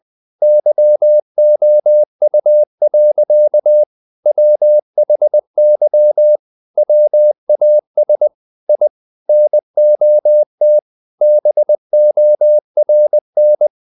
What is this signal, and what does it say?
Morse code, heard as a keyed tone, about 20 wpm, 600 Hz. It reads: YOU. WHY WAS I NOT BORN
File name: the old man and the sea part 2 20wpm